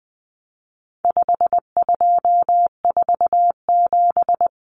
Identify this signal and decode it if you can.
Morse code, heard as a keyed tone, about 20 wpm, 700 Hz. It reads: 5247
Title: Common Cut Numbers - 4-Digit 20wpm